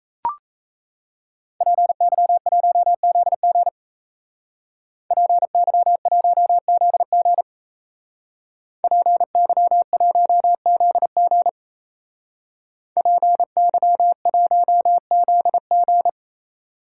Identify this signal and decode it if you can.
Morse code, heard as a keyed tone, about 37 wpm, 700 Hz. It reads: PY1ZG PY1ZG PY1ZG PY1ZG